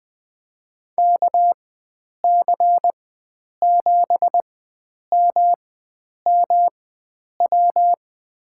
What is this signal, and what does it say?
Morse code, heard as a keyed tone, about 20 wpm, spaced as for 9 wpm, 700 Hz. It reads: KC7MMW
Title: USA Call Signs - 6 Characters - 4x Character Spacing - 20wpm